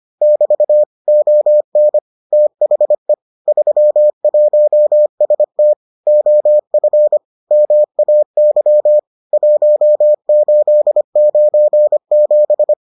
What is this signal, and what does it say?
Morse code, heard as a keyed tone, about 25 wpm, 600 Hz. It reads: <BT> ON THE 31ST OF MAY 1897